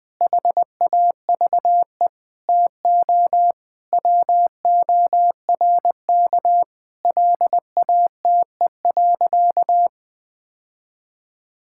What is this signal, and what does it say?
Morse code, heard as a keyed tone, about 20 wpm, 700 Hz. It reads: HAVE TO WORK LATE.
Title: Sentences with 5-Word Phrases - 20wpm